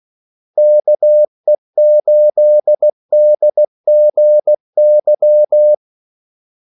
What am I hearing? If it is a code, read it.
Morse code, heard as a keyed tone, about 16 wpm, 600 Hz. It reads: KE8DGY
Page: Kaldesignaler 16 wpm